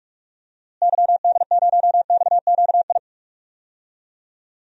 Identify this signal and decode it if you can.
Morse code, heard as a keyed tone, about 41 wpm, 700 Hz. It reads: YD0XQI
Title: DXCC-100 Call Signs - 6 Characters - Repeated 2x - 45wpm